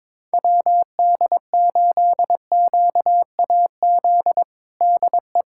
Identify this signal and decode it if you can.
Morse code, heard as a keyed tone, about 21 wpm, 700 Hz. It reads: WD8QAZ DE
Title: USA Callsign DE Callsign - 6 Characters - 22wpm